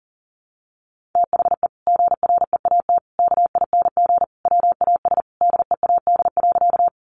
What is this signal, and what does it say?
Morse code, heard as a keyed tone, about 40 wpm, 700 Hz. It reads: THE GREAT KING WAS DEAD.
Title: Sentences from Top 1500 Words 40wpm